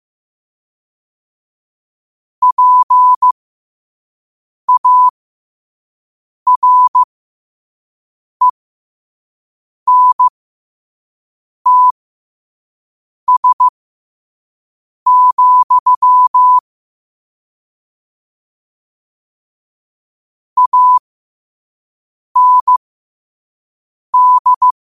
Morse code, heard as a keyed tone, about 15 wpm, spaced as for 5 wpm, 1000 Hz. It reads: PARENTS, AND
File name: qotdfiles / 5wpm